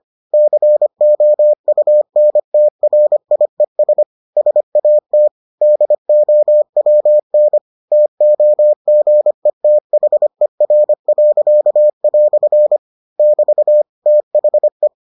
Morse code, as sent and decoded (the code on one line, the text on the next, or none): -.-. --- ..- -. - .-. .. ... / ... .- - / -.. --- .-- -. / - --- --. - .... .-. .-.-.- .-..-. / -...- / - .... .
COUNTRIES SAT DOWN TOGETHER." <BT> THE